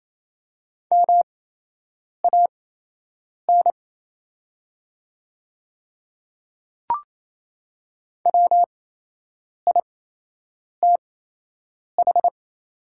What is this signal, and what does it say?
Morse code, heard as a keyed tone, about 28 wpm, spaced as for 8 wpm, 700 Hz. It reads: MAN WITH
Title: 8x Character Spacing - Top 200 Words - 28wpm